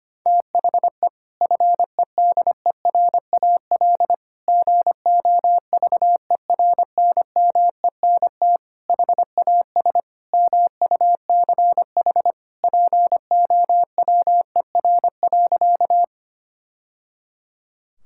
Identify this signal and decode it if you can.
Morse code, heard as a keyed tone, about 25 wpm, 700 Hz. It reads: THE FEDERAL GOVERNMENT HAS MUCH POWER.
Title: Sentences from Top 1000 Words - Review Part 5 - 25wpm